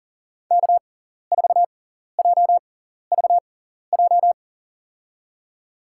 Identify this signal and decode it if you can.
Morse code, heard as a keyed tone, about 40 wpm, spaced as for 14 wpm, 700 Hz. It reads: K4JVJ
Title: USA Call Signs - 5 Characters - 6x Character Spacing - 40wpm